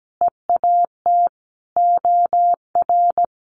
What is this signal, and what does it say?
Morse code, heard as a keyed tone, about 17 wpm, 700 Hz. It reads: EAT OR
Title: Sentences from Top 1500 Words 17wpm